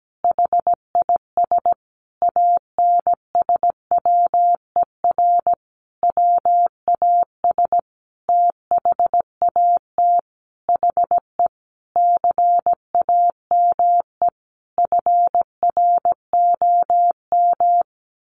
Morse code, heard as a keyed tone, about 17 wpm, 700 Hz. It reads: HIS ANSWER WAS THAT HE CAME FROM
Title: Sentences from Top 1500 Words 17wpm